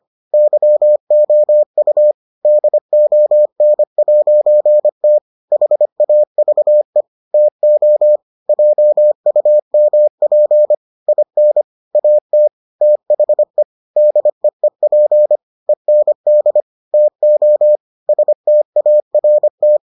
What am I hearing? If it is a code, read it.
Morse code, heard as a keyed tone, about 25 wpm, 600 Hz. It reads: YOU DON'T HAVE TO JUMP IN AT THE DEEP END TO START